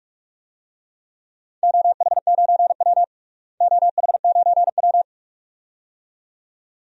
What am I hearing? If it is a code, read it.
Morse code, heard as a keyed tone, about 44 wpm, 700 Hz. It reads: OH9W OH9W
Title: DXCC-100 Call Signs - 4 Characters - Repeated 2x - 45wpm